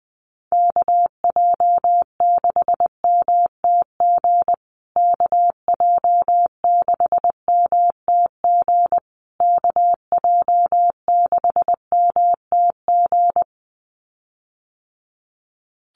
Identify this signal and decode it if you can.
Morse code, heard as a keyed tone, about 19 wpm, 700 Hz. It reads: KJ6MTG KJ6MTG KJ6MTG
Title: USA Call Signs Repeated 3x 20wpm